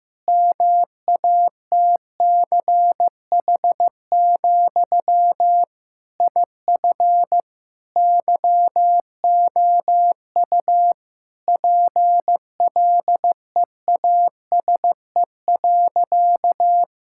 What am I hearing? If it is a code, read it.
Morse code, heard as a keyed tone, about 15 wpm, 700 Hz. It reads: MATCH, IF YOU PLEASE.
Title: Sentences from Top 1600 Words 15wpm